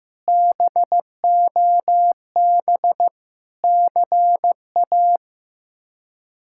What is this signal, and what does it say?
Morse code, heard as a keyed tone, about 15 wpm, 700 Hz. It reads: BOB CA
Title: K1USN SST QSO Practice - 15wpm